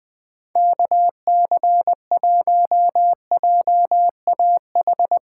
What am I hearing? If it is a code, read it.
Morse code, heard as a keyed tone, about 20 wpm, 700 Hz. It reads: KC1JAH
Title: USA Callsign DE Callsign 20wpm